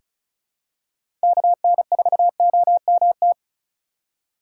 Morse code, heard as a keyed tone, about 36 wpm, 700 Hz. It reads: KN4OMT
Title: USA Call Signs - 6 Characters - 8x Character Spacing - 35wpm